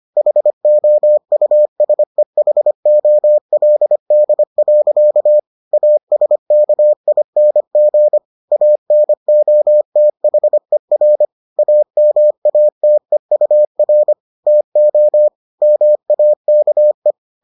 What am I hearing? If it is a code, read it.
Morse code, heard as a keyed tone, about 25 wpm, 600 Hz. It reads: HOUSEHOLD. ASKING ANOTHER AMATEUR TO MAKE